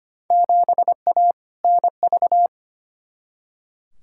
Morse code, heard as a keyed tone, about 25 wpm, 700 Hz. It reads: 7A NV
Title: Field Day Exchanges Encore - 25wpm